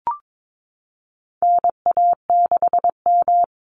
Morse code, heard as a keyed tone, about 22 wpm, 700 Hz. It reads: NA6M
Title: Call Signs Repeated 3x - 22wpm